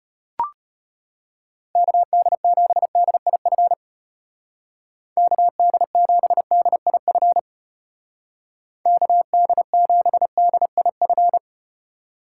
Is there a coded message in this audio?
Yes: Morse code from 1.8 to 11.4 s, heard as a keyed tone, about 35 wpm, 700 Hz. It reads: KD7DIF KD7DIF KD7DIF